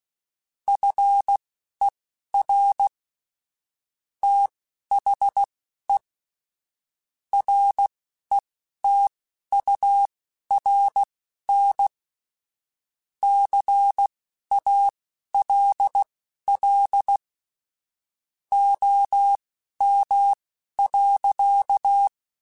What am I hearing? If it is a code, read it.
Morse code, heard as a keyed tone, about 16 wpm, spaced as for 11 wpm, 800 Hz. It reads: FER THE RETURN CALL OM.
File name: QSOs / QSOs-mp3s / kc7haa@10